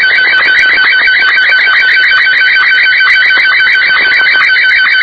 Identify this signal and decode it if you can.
Two-tone teleprinter signal, about 450 Hz shift.